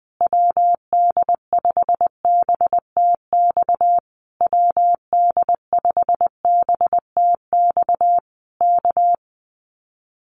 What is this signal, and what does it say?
Morse code, heard as a keyed tone, about 20 wpm, 700 Hz. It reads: WD5BTX WD5BTX K
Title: Common QSO Phrases Encore 20wpm